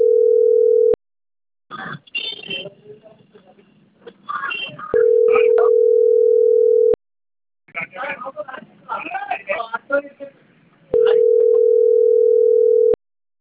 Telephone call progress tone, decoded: ringback tone